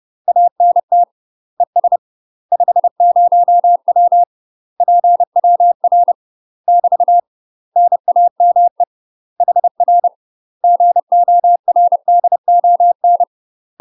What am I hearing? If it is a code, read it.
Morse code, heard as a keyed tone, about 30 wpm, 700 Hz. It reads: ANT ES 50W PWR <BT> NAME HR GORDON
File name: UK MASTER QSOs TRAINING / 30WPM / UK MASTER QSOs 0016